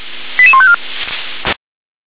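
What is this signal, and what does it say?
Five-tone selective call (ZVEI), heard as quick runs of five tones, tones of about 70 ms. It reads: 88154
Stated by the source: Five-tone ZVEI selective call received by SDR.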